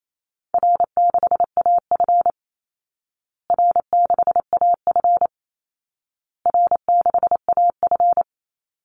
Morse code, heard as a keyed tone, about 28 wpm, 700 Hz. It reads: R6AF R6AF R6AF
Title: Easy Call Signs Repeated 3x - 28wpm